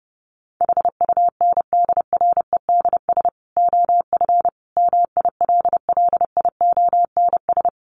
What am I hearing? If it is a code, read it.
Morse code, heard as a keyed tone, about 30 wpm, 700 Hz. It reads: HUNDREDS OF MILLIONS
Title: Sets of 5 Words 30wpm